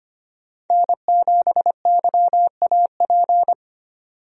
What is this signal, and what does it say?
Morse code, heard as a keyed tone, about 25 wpm, 690 Hz. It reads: N7YAP
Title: USA Call Signs - 5 Characters - 25wpm